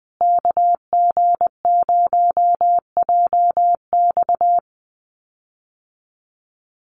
Morse code, heard as a keyed tone, about 20 wpm, 700 Hz. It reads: KG0JX
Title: USA Call Signs Repeated 3x 20wpm